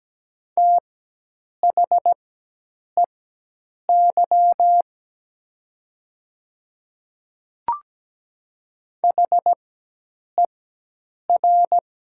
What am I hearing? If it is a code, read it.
Morse code, heard as a keyed tone, about 17 wpm, spaced as for 8 wpm, 700 Hz. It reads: THEY HER
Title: 4x Character Spacing - Top 200 Words - 17wpm